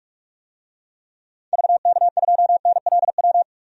Morse code, heard as a keyed tone, about 46 wpm, 700 Hz. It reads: VK2NLW
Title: DXCC-100 Call Signs - 6 Characters - 45wpm